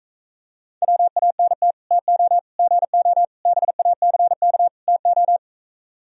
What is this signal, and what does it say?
Morse code, heard as a keyed tone, about 37 wpm, 700 Hz. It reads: WANT TO GO BACK TO